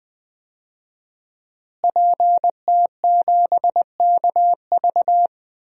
Morse code, heard as a keyed tone, about 20 wpm, 700 Hz. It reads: PT7KV
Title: DXCC-100 Call Signs - 5 Characters - Repeated 3x - 20wpm